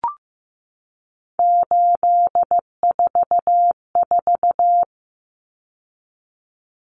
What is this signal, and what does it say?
Morse code, heard as a keyed tone, about 15 wpm, 700 Hz. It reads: 844
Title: Common Cut Numbers - 3-Digit 15wpm